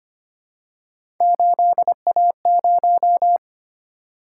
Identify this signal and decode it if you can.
Morse code, heard as a keyed tone, about 25 wpm, 700 Hz. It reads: 8A0
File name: Three Letter-Number Combinations - 25wpm